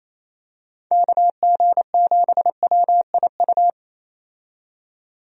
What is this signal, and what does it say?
Morse code, heard as a keyed tone, about 28 wpm, 700 Hz. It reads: KG7WIU
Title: USA Call Signs - 6 Characters - 28wpm